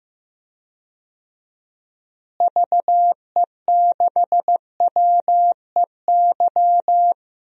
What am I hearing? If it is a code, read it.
Morse code, heard as a keyed tone, about 15 wpm, 700 Hz. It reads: VE6WEY
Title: DXCC-100 Call Signs - 6 Characters - Repeated 2x - 15wpm